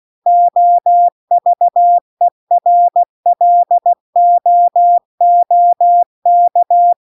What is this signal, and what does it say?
Morse code, heard as a keyed tone, about 16 wpm, 700 Hz. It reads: OVERLOOK